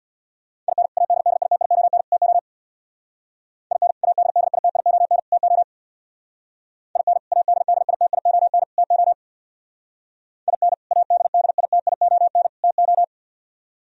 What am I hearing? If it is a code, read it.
Morse code, heard as a keyed tone, about 63 wpm, 700 Hz. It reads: IN ADDITION TO IN ADDITION TO IN ADDITION TO IN ADDITION TO